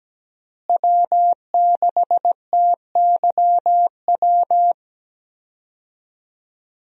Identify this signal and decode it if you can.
Morse code, heard as a keyed tone, about 17 wpm, 700 Hz. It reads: W6TYW
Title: USA Call Signs - 5 Characters - Repeated 2x - 17wpm